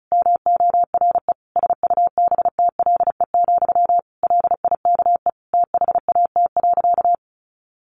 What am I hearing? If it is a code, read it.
Morse code, heard as a keyed tone, about 34 wpm, 700 Hz. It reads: MORE SUBTLE, LIKE THAT.
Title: A Sound Of Thunder 35wpm